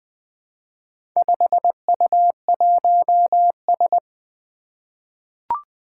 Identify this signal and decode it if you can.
Morse code, heard as a keyed tone, about 20 wpm, 700 Hz. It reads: 5U1S
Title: Four Challenging Letter-Number Combinations 20wpm